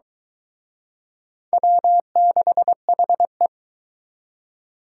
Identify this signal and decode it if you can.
Morse code, heard as a keyed tone, about 24 wpm, 700 Hz. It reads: W6HE